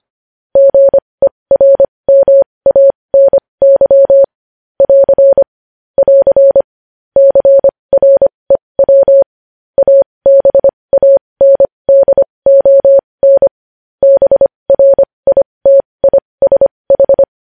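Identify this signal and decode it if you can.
Morse code, heard as a keyed tone, about 25 wpm, spaced as for 20 wpm, 550 Hz. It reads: GERMANY <AR> <AR> CREW ABANDON BRITISH
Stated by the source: Morse code news headlines sent at 20 words per minute.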